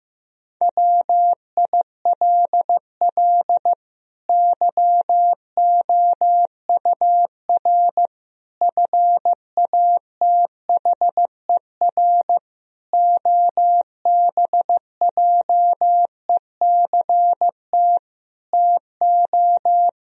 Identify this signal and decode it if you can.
Morse code, heard as a keyed tone, about 15 wpm, 700 Hz. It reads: WILL YOUR FATHER OBJECT TO
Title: Sentences from Top 1000 Words - Review Part 3 - 15wpm